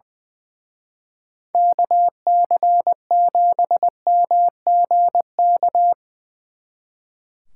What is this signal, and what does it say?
Morse code, heard as a keyed tone, about 22 wpm, 700 Hz. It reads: KC7MGK
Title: Speed Racing - USA 2x3 Call Signs 20wpm